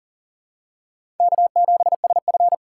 Morse code, heard as a keyed tone, about 40 wpm, 700 Hz. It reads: K7SF
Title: K1USN SST QSO Practice - 40wpm